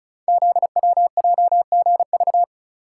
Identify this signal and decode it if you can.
Morse code, heard as a keyed tone, about 36 wpm, 690 Hz. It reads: ZWJGV
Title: Code Groups - 5 Letters - 35wpm